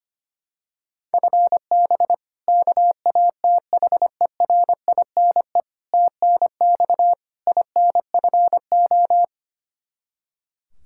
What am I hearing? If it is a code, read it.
Morse code, heard as a keyed tone, about 25 wpm, 700 Hz. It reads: FB KATHERINE TNX INFO
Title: K1USN SST QSO Practice 25wpm